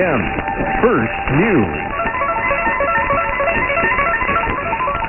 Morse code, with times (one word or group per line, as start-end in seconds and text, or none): none